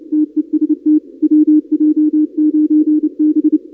Morse code, heard as a keyed tone, about 28 wpm, 300 Hz. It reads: TEST WJ9B
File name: contests / 2019-cw-wpx-kt5j-k5pi- -2330 0-31